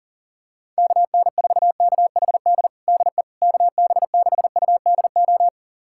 Morse code, heard as a keyed tone, about 40 wpm, 700 Hz. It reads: KN4KHD DE KB6UDO
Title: USA Callsign DE Callsign 40wpm